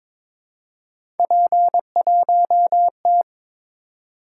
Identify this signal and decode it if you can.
Morse code, heard as a keyed tone, about 22 wpm, 700 Hz. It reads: P1T
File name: practice / 22wpm / Three Letter-Number Combinations - 22wpm